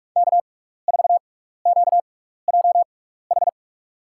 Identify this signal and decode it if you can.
Morse code, heard as a keyed tone, about 47 wpm, spaced as for 16 wpm, 700 Hz. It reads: K4QJH
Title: USA Call Signs - 5 Characters - 6x Character Spacing - 45wpm